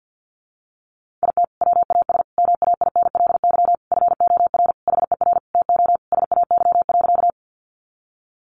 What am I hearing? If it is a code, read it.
Morse code, heard as a keyed tone, about 49 wpm, 700 Hz. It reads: IT WAS MAINLY FOR HER TO SAY.
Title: Sentences from Top 1400 Words 50wpm